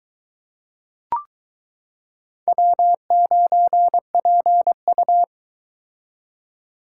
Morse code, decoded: W9PU

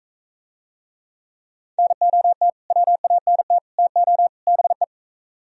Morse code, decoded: NOT WANT TO BE

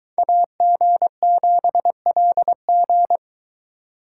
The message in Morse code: AG7LG

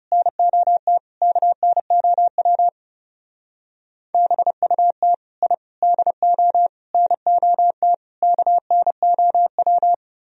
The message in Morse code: NOT KNOW BUT I DO NOT KNOW

